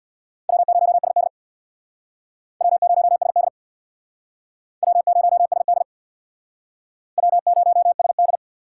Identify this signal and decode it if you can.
Morse code, heard as a keyed tone, about 61 wpm, 700 Hz. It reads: W0SD W0SD W0SD W0SD
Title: Speed Racing - USA 1x2 Call Signs 50wpm